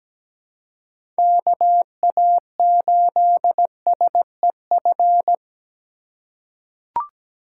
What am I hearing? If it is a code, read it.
Morse code, heard as a keyed tone, about 17 wpm, 700 Hz. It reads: KA8SEF
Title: USA Call Signs - 6 Characters - 3x Character Spacing - 17wpm